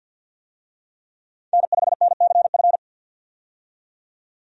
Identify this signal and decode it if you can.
Morse code, heard as a keyed tone, about 53 wpm, 700 Hz. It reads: N5NCF
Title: USA Call Signs - 5 Characters - 50wpm